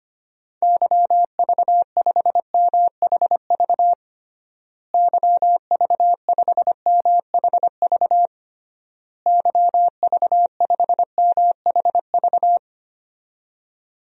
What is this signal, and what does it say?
Morse code, heard as a keyed tone, about 25 wpm, 700 Hz. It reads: YV5MHV YV5MHV YV5MHV
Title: DXCC-100 Call Signs - 6 Characters - Repeated 3x - 25wpm